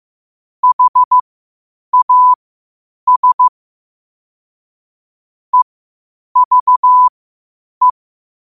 Morse code, heard as a keyed tone, about 15 wpm, spaced as for 8 wpm, 1000 Hz. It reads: HAS EVE
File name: qotdfiles / 8wpm